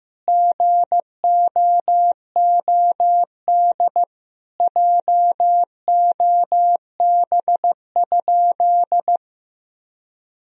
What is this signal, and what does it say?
Morse code, heard as a keyed tone, about 15 wpm, 700 Hz. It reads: GOOD JOB?
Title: Sentences from Top 1000 Words - Review Part 3 - 15wpm